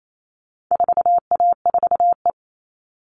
Morse code, heard as a keyed tone, about 28 wpm, 700 Hz. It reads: 4A4E
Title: All Cut Numbers - 4-Digit 28wpm